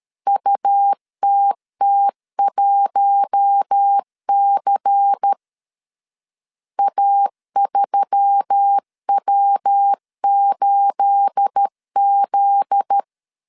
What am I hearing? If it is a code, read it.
Morse code, heard as a keyed tone, about 13 wpm, 800 Hz. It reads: UTT1C A3W8Z
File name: CW / DET / 1 9 1500C S A 12wpm 3800